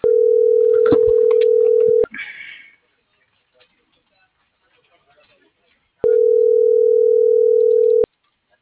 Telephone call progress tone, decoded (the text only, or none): ringback tone